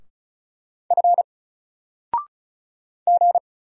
Morse code, heard as a keyed tone, about 35 wpm, 700 Hz. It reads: F G
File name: Single-Letter-Number-35wpm